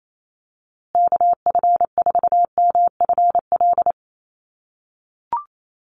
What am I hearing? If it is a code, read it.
Morse code, heard as a keyed tone, about 28 wpm, 700 Hz. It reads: KF4MFL
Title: USA Call Signs Repeated 2x 28wpm